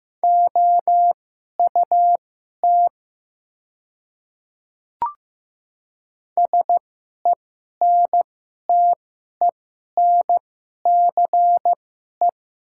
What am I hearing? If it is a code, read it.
Morse code, heard as a keyed tone, about 15 wpm, spaced as for 11 wpm, 700 Hz. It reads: OUT SENTENCE
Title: 2x Character Spacing - Top 200 Words - 15wpm